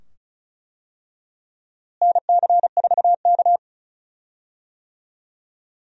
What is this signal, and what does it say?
Morse code, heard as a keyed tone, about 36 wpm, 700 Hz. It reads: NC4K
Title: USA Call Signs - 4 Characters - Repeated 2x - 35wpm